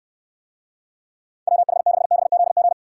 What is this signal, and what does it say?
Morse code, heard as a keyed tone, about 68 wpm, 700 Hz. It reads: WH6BZG